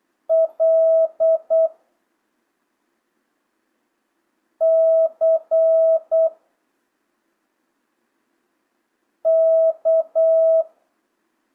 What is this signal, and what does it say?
Morse code, heard as a keyed tone, about 8 wpm, 650 Hz. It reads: L C K
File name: jh8cbh / 2025cw